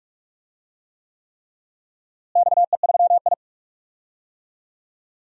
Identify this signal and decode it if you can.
Morse code, heard as a keyed tone, about 46 wpm, 700 Hz. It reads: XE3I
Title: DXCC-100 Call Signs - 4 Characters - Repeated 3x - 45wpm